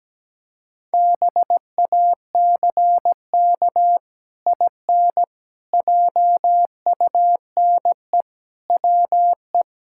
Morse code, heard as a keyed tone, about 17 wpm, 700 Hz. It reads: BACK IN JUNE WE
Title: Pangram Challenge v2 - 17wpm